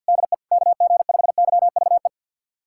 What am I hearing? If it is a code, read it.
Morse code, heard as a keyed tone, about 52 wpm, 700 Hz. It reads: DE KG5YVE
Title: USA Callsign DE Callsign - 6 Characters - 50wpm